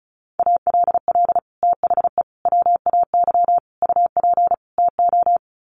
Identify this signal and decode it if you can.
Morse code, heard as a keyed tone, about 34 wpm, 700 Hz. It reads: ALL THE WAY UP TO
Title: Sets of 5 Words Encore 35wpm